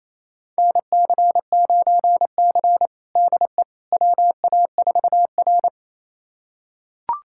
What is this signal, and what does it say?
Morse code, heard as a keyed tone, about 27 wpm, 700 Hz. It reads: NC9C DE WA4R
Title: USA Callsign DE Callsign - 4 Characters - 28wpm